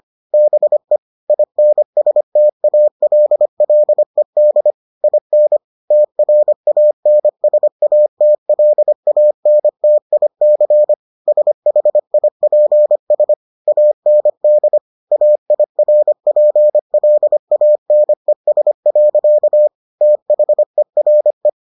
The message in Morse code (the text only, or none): BE INSTALLED IN TRANSATLANTIC SHIPS AND AIRPLANES. THERE